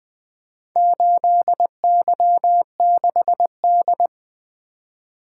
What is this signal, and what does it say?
Morse code, heard as a keyed tone, about 20 wpm, 700 Hz. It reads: 8Y6D
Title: Four Letter-Number Combinations - 20wpm